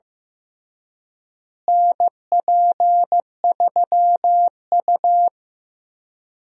Morse code, heard as a keyed tone, about 16 wpm, 700 Hz. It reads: NP3U
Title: USA Call Signs - 4 Characters - Repeated 2x - 15wpm